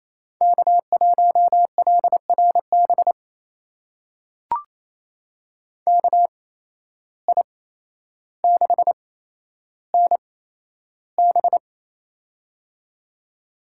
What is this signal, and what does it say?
Morse code, heard as a keyed tone, about 28 wpm, spaced as for 12 wpm, 700 Hz. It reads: K1LRB KI6NB
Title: USA Call Signs - 5 Characters - 8x Character Spacing - 28wpm